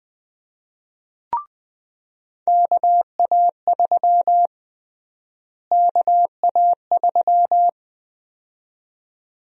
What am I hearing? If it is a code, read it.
Morse code, heard as a keyed tone, about 20 wpm, 700 Hz. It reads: KA3 KA3
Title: USA Call Sign Prefixes - Repeated 2x - 20wpm